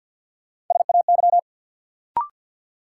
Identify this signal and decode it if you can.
Morse code, heard as a keyed tone, about 53 wpm, 700 Hz. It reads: SAY
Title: Sentences from Top 200 Words - New Word Drill - 50wpm